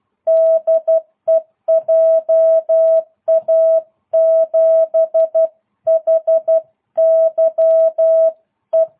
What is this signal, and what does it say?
Morse code, heard as a keyed tone, about 12 wpm, 660 Hz. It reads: DE JA7HYE